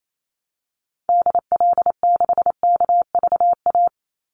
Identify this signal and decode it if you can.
Morse code, heard as a keyed tone, about 28 wpm, 700 Hz. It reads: DL6KVA
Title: Call Signs Repeated 2x - 28wpm